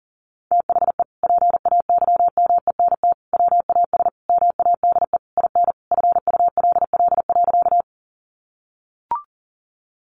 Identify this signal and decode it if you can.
Morse code, heard as a keyed tone, about 40 wpm, 700 Hz. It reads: THE PAYMENT WAS MADE IN FULL.
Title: Sentences from Top 1200 Words 40wpm